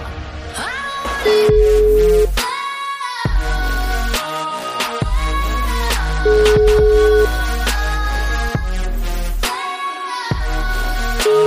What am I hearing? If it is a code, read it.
Ringback tone (Europe, 425 Hz), with noise.